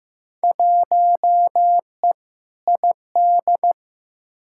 Morse code, heard as a keyed tone, about 15 wpm, 700 Hz. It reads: 1E ID
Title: Field Day Exchanges - 15wpm